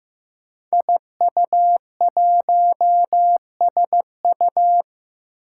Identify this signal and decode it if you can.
Morse code, heard as a keyed tone, about 15 wpm, 700 Hz. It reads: IU1SU
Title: Five Challenging Letter-Number Combinations 15wpm